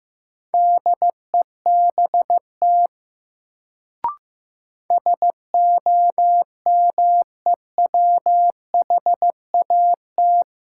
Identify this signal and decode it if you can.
Morse code, heard as a keyed tone, about 15 wpm, 700 Hz. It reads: DEBT SOMEWHAT
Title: Sentences from Top 1300 Words - New Word Drill Alt - 15wpm